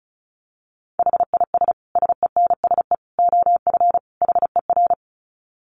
Morse code, heard as a keyed tone, about 34 wpm, 700 Hz. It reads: HIS SENSE OF HER
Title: Dit Challenge 35wpm